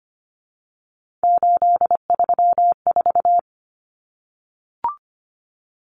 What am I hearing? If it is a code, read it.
Morse code, heard as a keyed tone, about 25 wpm, 700 Hz. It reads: 834
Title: Common Cut Numbers - 3-Digit 25wpm